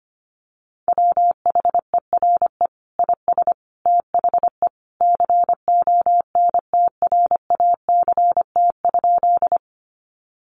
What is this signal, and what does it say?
Morse code, heard as a keyed tone, about 25 wpm, 700 Hz. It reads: WHERE IS THE CONTRACT?
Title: Sentences from Top 1400 Words 25wpm